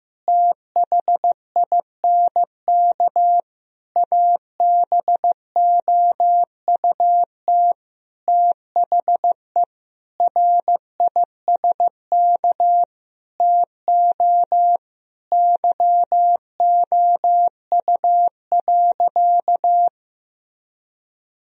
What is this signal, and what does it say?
Morse code, heard as a keyed tone, about 15 wpm, 700 Hz. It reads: THINK ABOUT THE RISK TO YOU.
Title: Sentences from Top 1000 Words - Review Part 4 - 15wpm